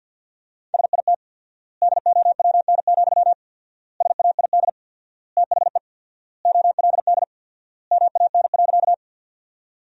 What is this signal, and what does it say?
Morse code, heard as a keyed tone, about 49 wpm, 700 Hz. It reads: SIT DOWN, SAID THE OLD MAN.
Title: 4x Word-Spacing - Sentences from Top 500 Words - 50wpm